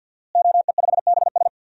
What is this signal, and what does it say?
Morse code, heard as a keyed tone, about 46 wpm, 700 Hz. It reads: OE5BS
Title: DXCC-100 Call Signs - 5 Characters - Repeated 3x - 50wpm